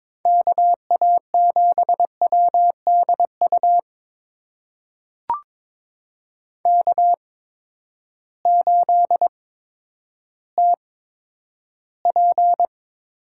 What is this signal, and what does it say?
Morse code, heard as a keyed tone, about 22 wpm, spaced as for 11 wpm, 700 Hz. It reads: KA7WDU K8TP
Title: USA Call Signs - 8x Character Spacing - 22wpm